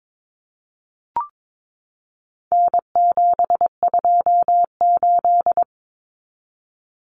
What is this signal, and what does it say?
Morse code, heard as a keyed tone, about 22 wpm, 700 Hz. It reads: N728
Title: Common Cut Numbers - 4-Digit 22wpm